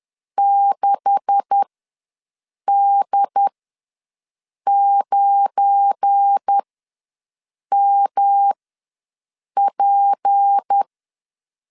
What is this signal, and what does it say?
Morse code, heard as a keyed tone, about 10 wpm, spaced as for 6 wpm, 800 Hz. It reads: 6D9MP
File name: LEZIONI / 9 286 ws 10wpm 4032